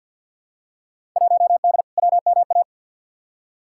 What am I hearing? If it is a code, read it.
Morse code, heard as a keyed tone, about 52 wpm, 700 Hz. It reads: 1D WMA
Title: Field Day Exchanges - 50wpm